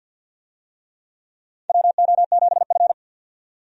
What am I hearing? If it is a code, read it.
Morse code, heard as a keyed tone, about 52 wpm, 700 Hz. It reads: WO7F
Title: USA Call Signs - 4 Characters - 50wpm